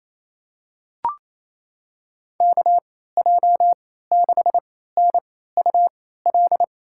Morse code, heard as a keyed tone, about 28 wpm, spaced as for 16 wpm, 700 Hz. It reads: KJ6NUL
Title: USA Call Signs - 6 Characters - 3x Character Spacing - 28wpm